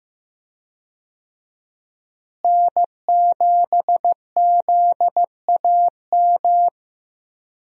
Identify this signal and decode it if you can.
Morse code, heard as a keyed tone, about 15 wpm, 700 Hz. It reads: N7ZAM
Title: USA Call Signs - 5 Characters - Repeated 3x - 15wpm